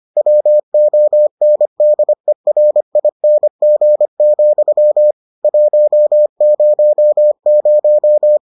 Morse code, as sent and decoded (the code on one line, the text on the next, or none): .-- --- -. -.. .-. .. -. --. --..-- / .---- ----- -----
WONDERING, 100